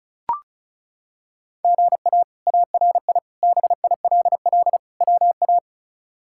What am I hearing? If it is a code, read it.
Morse code, heard as a keyed tone, about 36 wpm, 700 Hz. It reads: GA ARI BILL WA
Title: K1USN SST QSO Practice - 35wpm